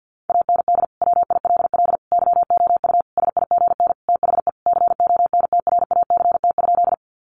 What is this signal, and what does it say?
Morse code, heard as a keyed tone, about 50 wpm, 700 Hz. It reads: AND WILL YOU SIGN THE CONTRACT?
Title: Sentences from Top 1400 Words 50wpm